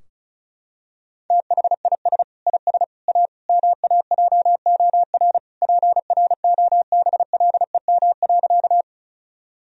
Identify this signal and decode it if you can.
Morse code, heard as a keyed tone, about 36 wpm, 700 Hz. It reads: THIS IS A MAJOR PROBLEM.
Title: Sentences from Top 1000 Words - Review Part 4 - 35wpm